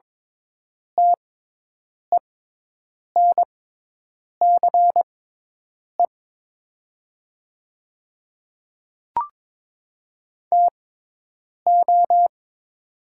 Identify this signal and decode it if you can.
Morse code, heard as a keyed tone, about 23 wpm, spaced as for 8 wpm, 700 Hz. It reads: TENCE TO
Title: 6x Character Spacing - Top 200 Words - 22wpm